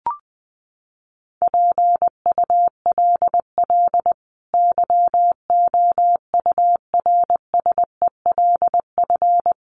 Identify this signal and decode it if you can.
Morse code, heard as a keyed tone, about 20 wpm, 700 Hz. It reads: PULL YOURSELF
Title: Sentences from Top 600 Words 20wpm